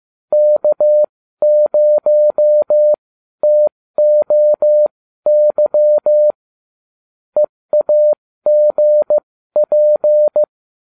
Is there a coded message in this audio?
Yes: Morse code from 0.3 to 10.4 s, heard as a keyed tone, about 15 wpm, spaced as for 12 wpm, 600 Hz. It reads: K0TOY EAGP